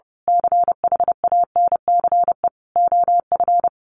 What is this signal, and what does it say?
Morse code, heard as a keyed tone, about 30 wpm, 700 Hz. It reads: CHANCE OF
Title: Sentences from Top 1300 Words 30wpm